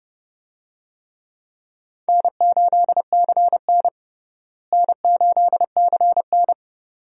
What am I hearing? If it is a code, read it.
Morse code, heard as a keyed tone, about 31 wpm, 700 Hz. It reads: N8CN N8CN
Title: USA Call Signs - 4 Characters - Repeated 2x - 30wpm